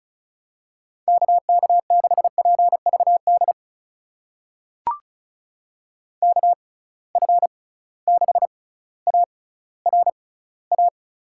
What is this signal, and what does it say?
Morse code, heard as a keyed tone, about 34 wpm, spaced as for 18 wpm, 700 Hz. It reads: KK6PVD KF6ARA
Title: USA Call Signs - 6 Characters - 6x Character Spacing - 35wpm